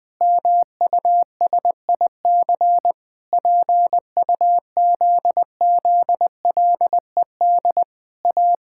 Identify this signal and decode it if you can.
Morse code, heard as a keyed tone, about 19 wpm, 700 Hz. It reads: MUSIC PUZZLED A
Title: Pangram Challenge v2 - 20wpm